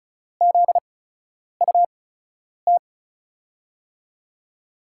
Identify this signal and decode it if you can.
Morse code, heard as a keyed tone, about 36 wpm, spaced as for 10 wpm, 700 Hz. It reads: ZUT
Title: USA Call Signs - 5 Characters - 8x Character Spacing - 35wpm